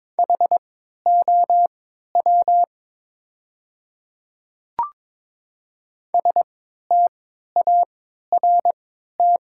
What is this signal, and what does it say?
Morse code, heard as a keyed tone, about 22 wpm, spaced as for 12 wpm, 700 Hz. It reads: HOW START
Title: 3x Character Spacing - Top 300 Words - 22wpm